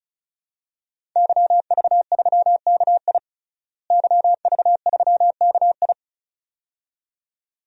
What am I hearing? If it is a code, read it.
Morse code, heard as a keyed tone, about 34 wpm, 700 Hz. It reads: YV3KI YV3KI
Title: DXCC-100 Call Signs - 5 Characters - Repeated 2x - 35wpm